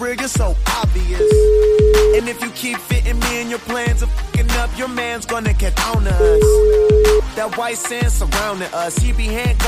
Ringback tone (Europe, 425 Hz), with noise.